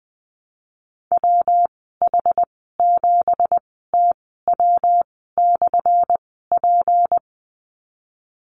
Morse code, heard as a keyed tone, about 20 wpm, spaced as for 15 wpm, 700 Hz. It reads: WH7TW/P
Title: 179 Lesson 29g - Slash - Call Signs 20wpm